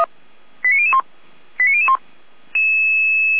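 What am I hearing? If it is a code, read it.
Five-tone selective call (ZVEI), heard as quick runs of five tones, tones of about 70 ms. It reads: 89001 89001
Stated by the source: ZVEI five-tone selective call received on SDR.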